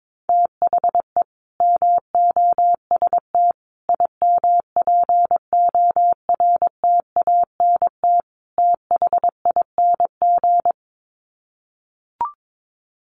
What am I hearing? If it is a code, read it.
Morse code, heard as a keyed tone, about 22 wpm, 700 Hz. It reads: THE MOST IMPORTANT THING
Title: Sets of 4 Words 22wpm